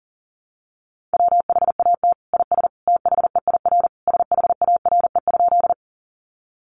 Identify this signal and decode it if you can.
Morse code, heard as a keyed tone, about 40 wpm, 700 Hz. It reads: WHAT IS THEIR SHARE?
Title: Sentences from Top 900 Words 40wpm